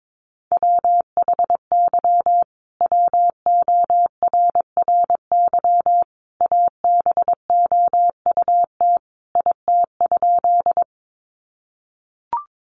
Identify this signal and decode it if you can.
Morse code, heard as a keyed tone, about 22 wpm, 700 Hz. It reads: WHY WORRY ABOUT IT?
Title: Sentences from Top 1100 Words 22wpm